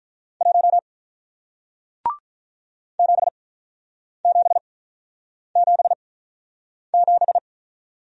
Morse code, heard as a keyed tone, about 48 wpm, spaced as for 9 wpm, 700 Hz. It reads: 1 7777